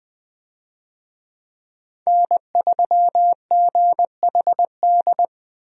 Morse code, heard as a keyed tone, about 20 wpm, 700 Hz. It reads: N3GHD
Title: USA Call Signs - 5 Characters - 20wpm